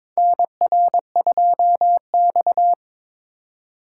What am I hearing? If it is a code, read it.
Morse code, heard as a keyed tone, about 22 wpm, 700 Hz. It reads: NR2X
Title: USA Call Signs - 4 Characters - 3x Character Spacing - 22wpm